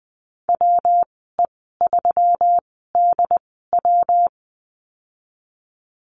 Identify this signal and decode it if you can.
Morse code, heard as a keyed tone, about 19 wpm, spaced as for 14 wpm, 700 Hz. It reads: WE3DW
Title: 140 Lesson 23g - Letter f - Call Signs 20wpm